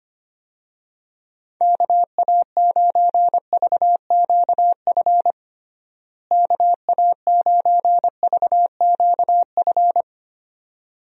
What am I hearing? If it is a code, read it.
Morse code, heard as a keyed tone, about 25 wpm, 700 Hz. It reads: KA9VQF KA9VQF
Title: USA Call Signs - 6 Characters - Repeated 2x - 25wpm